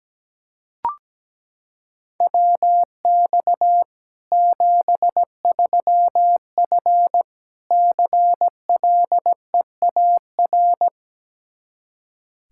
Morse code, heard as a keyed tone, about 17 wpm, 700 Hz. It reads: WX 73F CLEAR
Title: Practice QSO - Exchange 2 - Info - 17wpm